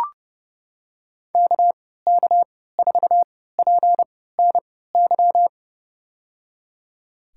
Morse code, heard as a keyed tone, about 31 wpm, spaced as for 17 wpm, 700 Hz. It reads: KK4PNY
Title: USA Call Signs - 6 Characters - 3x Character Spacing - 30wpm